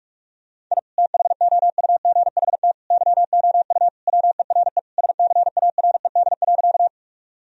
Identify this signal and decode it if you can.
Morse code, heard as a keyed tone, about 44 wpm, 700 Hz. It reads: I THOUGHT YOU WERE SCARED.
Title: Sentences from Top 1000 Words - Review Part 4 - 45wpm